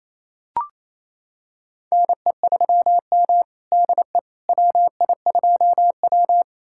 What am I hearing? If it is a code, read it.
Morse code, heard as a keyed tone, about 28 wpm, 700 Hz. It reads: NE3M DE WI2W
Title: USA Callsign DE Callsign - 4 Characters - 28wpm